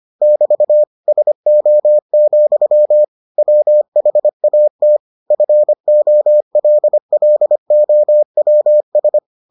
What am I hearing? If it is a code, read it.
Morse code, heard as a keyed tone, about 25 wpm, 600 Hz. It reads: <BT> SO, WHAT FOLLOWS